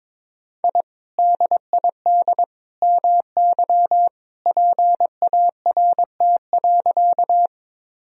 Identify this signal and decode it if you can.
Morse code, heard as a keyed tone, about 23 wpm, 700 Hz. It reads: I DID MY PART.
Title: Sentences from Top 1000 Words - Review Part 1 - 22wpm